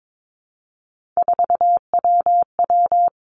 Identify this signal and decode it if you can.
Morse code, heard as a keyed tone, about 22 wpm, 700 Hz. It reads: 4WW
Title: Three Challenging Letter-Number Combinations 22wpm